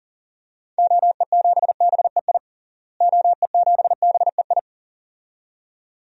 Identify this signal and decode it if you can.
Morse code, heard as a keyed tone, about 37 wpm, 700 Hz. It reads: OE7BEI OE7BEI
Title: DXCC-100 Call Signs - 6 Characters - Repeated 2x - 40wpm